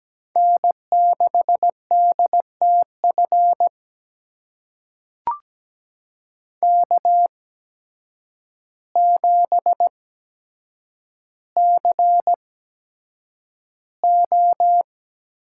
Morse code, heard as a keyed tone, about 17 wpm, spaced as for 8 wpm, 700 Hz. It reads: N6DTF K7CO
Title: USA Call Signs - 5 Characters - 8x Character Spacing - 17wpm